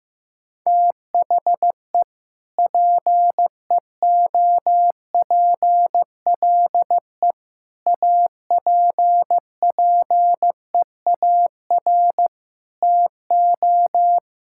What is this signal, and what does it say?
Morse code, heard as a keyed tone, about 15 wpm, 700 Hz. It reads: THE PEOPLE APPEAR TO
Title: Sentences from Top 1000 Words - Review Part 5 - 15wpm